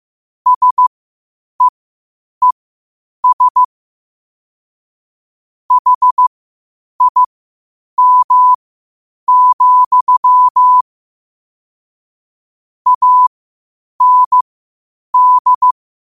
Morse code, heard as a keyed tone, about 15 wpm, spaced as for 8 wpm, 1000 Hz. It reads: SEES HIM, AND
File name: qotdfiles / 8wpm